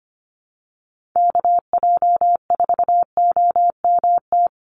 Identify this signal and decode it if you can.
Morse code, heard as a keyed tone, about 25 wpm, 700 Hz. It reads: KJ4OMT
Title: USA Call Signs 25wpm